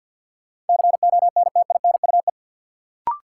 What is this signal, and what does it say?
Morse code, heard as a keyed tone, about 53 wpm, 700 Hz. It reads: CONTINUE
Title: Sentences from Top 1000 Words - Review Part 6 - New Word Drill - 50wpm